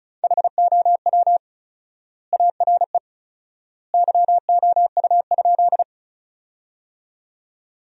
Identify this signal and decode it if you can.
Morse code, heard as a keyed tone, about 36 wpm, 700 Hz. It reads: HOW ARE YOU?
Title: 4x Word-Spacing - Sentences from Top 100 Words - 35wpm